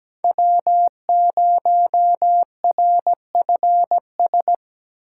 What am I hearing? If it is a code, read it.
Morse code, heard as a keyed tone, about 17 wpm, 700 Hz. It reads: W0RFS
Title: USA Call Signs - 2x Character Spacing - 17wpm